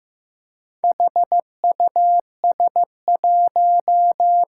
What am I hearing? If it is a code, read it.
Morse code, heard as a keyed tone, about 15 wpm, 700 Hz. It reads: HUS1
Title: Four Challenging Letter-Number Combinations 15wpm